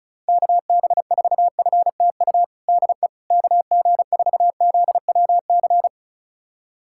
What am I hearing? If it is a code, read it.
Morse code, heard as a keyed tone, about 34 wpm, 700 Hz. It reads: KB4FTU DE KG4ZWC
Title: USA Callsign DE Callsign - 6 Characters - 35wpm